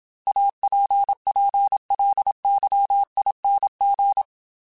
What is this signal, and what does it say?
Morse code, heard as a keyed tone, about 27 wpm, 800 Hz. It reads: APPLYING
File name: Technical Subjects / Technical Subjects-mp3s / Sw Pwr Sup @25